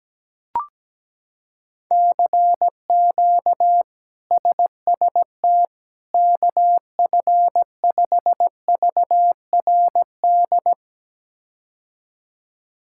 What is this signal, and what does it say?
Morse code, heard as a keyed tone, about 17 wpm, 700 Hz. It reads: CQ SST KF5VRD
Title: K1USN SST QSO Practice - 17wpm